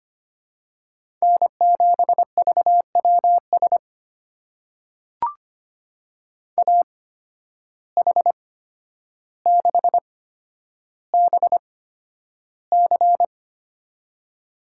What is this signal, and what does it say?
Morse code, heard as a keyed tone, about 25 wpm, spaced as for 11 wpm, 700 Hz. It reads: N7VWS AH6BC
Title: USA Call Signs - 5 Characters - 8x Character Spacing - 25wpm